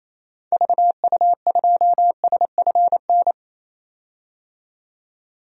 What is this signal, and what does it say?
Morse code, heard as a keyed tone, about 28 wpm, 700 Hz. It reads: VU2SFN
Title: DXCC-100 Call Signs - 6 Characters - 28wpm